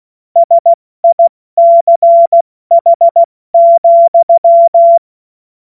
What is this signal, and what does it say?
Morse code, heard as a keyed tone, about 16 wpm, 680 Hz. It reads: SICH,
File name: Morsekurs / Witze / Witz 20. September 2022 70 F80